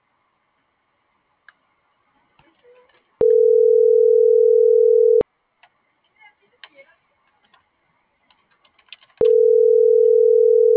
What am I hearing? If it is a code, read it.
Ringback tone (North America), 440 + 480 Hz.